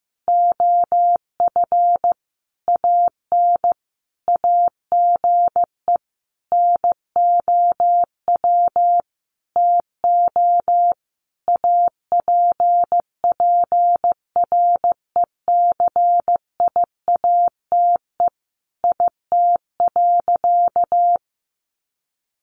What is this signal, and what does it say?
Morse code, heard as a keyed tone, about 15 wpm, 700 Hz. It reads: OF AN AGE NOW TO APPRECIATE IT.
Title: Sentences from Top 1300 Words 15wpm